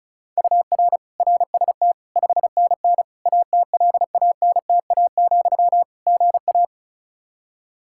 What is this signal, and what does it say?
Morse code, heard as a keyed tone, about 36 wpm, 700 Hz. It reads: UR RST 5NN ATLANTA, GA
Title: Practice QSO - Exchange 1 - Beginning - 35wpm